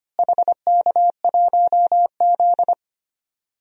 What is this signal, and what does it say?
Morse code, heard as a keyed tone, about 25 wpm, 700 Hz. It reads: HK1Z
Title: DXCC-100 Call Signs - 4 Characters - Repeated 3x - 25wpm